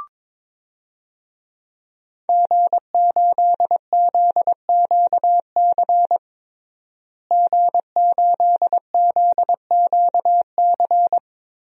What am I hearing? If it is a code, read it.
Morse code, heard as a keyed tone, about 22 wpm, 700 Hz. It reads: G8ZQC G8ZQC
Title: DXCC-100 Call Signs - 5 Characters - Repeated 3x - 22wpm